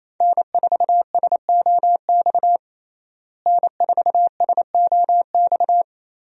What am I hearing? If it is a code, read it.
Morse code, heard as a keyed tone, about 28 wpm, 700 Hz. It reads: N4SOX N4SOX
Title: USA Call Signs - 5 Characters - Repeated 3x - 28wpm